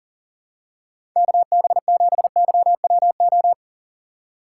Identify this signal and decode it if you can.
Morse code, heard as a keyed tone, about 39 wpm, 700 Hz. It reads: KB7YWO
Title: USA Call Signs - 6 Characters - 4x Character Spacing - 40wpm